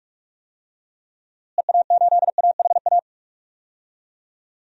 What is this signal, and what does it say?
Morse code, heard as a keyed tone, about 43 wpm, 700 Hz. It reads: EA8AHA
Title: DXCC-100 Call Signs - 6 Characters - Repeated 2x - 45wpm